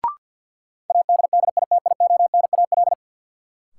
Morse code, heard as a keyed tone, about 48 wpm, 700 Hz. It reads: ADDITIONAL
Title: Sentences from Top 1400 Words - New Word Drill Alt - 50wpm